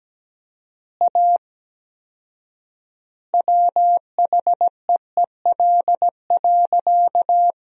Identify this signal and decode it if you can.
Morse code, heard as a keyed tone, about 17 wpm, 700 Hz. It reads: A WHEEL.
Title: 4x Word-Spacing - Sentences from Top 500 Words - 17wpm